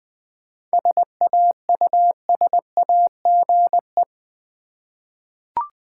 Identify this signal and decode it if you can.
Morse code, heard as a keyed tone, about 20 wpm, 690 Hz. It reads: SAUSAGE
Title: Difficult Words with Hints - 20wpm